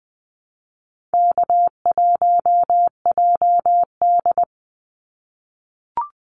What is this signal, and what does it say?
Morse code, heard as a keyed tone, about 20 wpm, 700 Hz. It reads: K1JD
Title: NA SOTA Activator Callsigns 20wpm